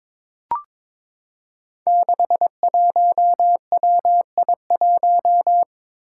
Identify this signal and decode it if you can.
Morse code, heard as a keyed tone, about 21 wpm, 700 Hz. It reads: 61WI1
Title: Five Challenging Letter-Number Combinations 22wpm